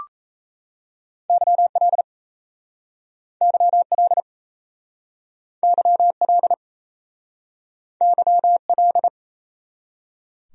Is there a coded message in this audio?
Yes: Morse code from 1.3 to 9.1 s, heard as a keyed tone, about 37 wpm, 700 Hz. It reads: YL YL YL YL